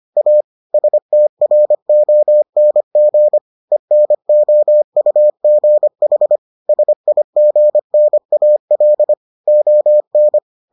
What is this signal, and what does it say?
Morse code, heard as a keyed tone, about 25 wpm, 600 Hz. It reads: A STRONG ENOUGH SIGNAL ON